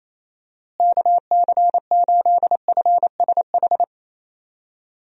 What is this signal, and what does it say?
Morse code, heard as a keyed tone, about 28 wpm, 700 Hz. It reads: KC8FSH
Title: USA Call Signs - 6 Characters - Repeated 2x - 28wpm